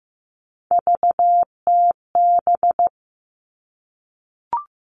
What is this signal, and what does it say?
Morse code, heard as a keyed tone, about 15 wpm, 700 Hz. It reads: VTB
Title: All Cut Numbers - 3-Digit 15wpm